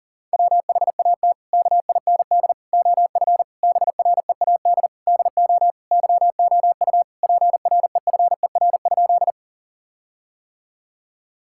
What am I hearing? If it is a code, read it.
Morse code, heard as a keyed tone, about 40 wpm, 700 Hz. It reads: WHAT KIND OF BREAD DO YOU PREFER?
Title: Sentences from Top 1600 Words 40wpm